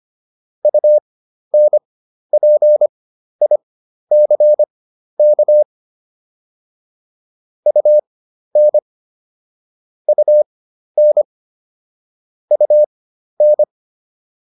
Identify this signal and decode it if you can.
Morse code, heard as a keyed tone, about 25 wpm, spaced as for 12 wpm, 600 Hz. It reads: UNPICK UN UN UN
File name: UN6 12